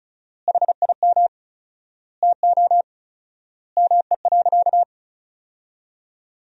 Morse code, heard as a keyed tone, about 34 wpm, 700 Hz. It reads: HIM TO ME.
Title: 4x Word-Spacing - Sentences from Top 200 Words - 35wpm